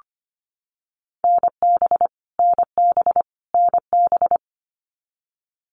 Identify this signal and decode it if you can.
Morse code, heard as a keyed tone, about 25 wpm, 700 Hz. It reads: NB NB NB
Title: Canadian Provinces and Territories - Repeated 3x - 25wpm